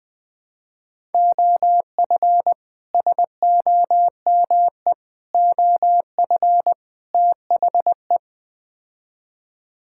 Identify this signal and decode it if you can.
Morse code, heard as a keyed tone, about 21 wpm, 700 Hz. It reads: OF SOME OF THE
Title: Build A Phrase - 4-Word Phrases - 20wpm